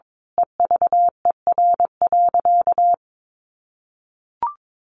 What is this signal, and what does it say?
Morse code, heard as a keyed tone, about 22 wpm, 700 Hz. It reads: EVER.
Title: Sentences from Top 1200 Words 22wpm